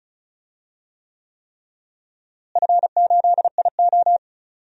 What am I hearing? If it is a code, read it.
Morse code, heard as a keyed tone, about 34 wpm, 700 Hz. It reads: F8IO
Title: DXCC-100 Call Signs - 4 Characters - Repeated 3x - 35wpm